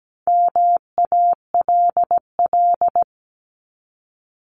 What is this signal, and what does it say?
Morse code, heard as a keyed tone, about 17 wpm, 700 Hz. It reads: MALL
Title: Top 1501-1750 Words 17wpm